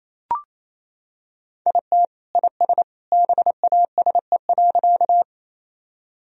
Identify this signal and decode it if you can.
Morse code, heard as a keyed tone, about 28 wpm, 700 Hz. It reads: IT IS BASE.
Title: Sentences from Top 1000 Words - Review Part 1 - 28wpm